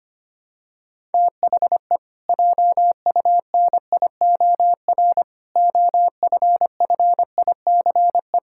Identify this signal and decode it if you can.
Morse code, heard as a keyed tone, about 25 wpm, 700 Hz. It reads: THE JUNIOR OFFICE
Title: Pangram Challenge v2 - 25wpm